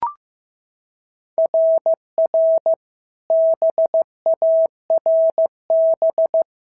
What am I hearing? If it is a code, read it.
Morse code, heard as a keyed tone, about 14 wpm, 650 Hz. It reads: RR BARB
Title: SOTA QSO Practice 15wpm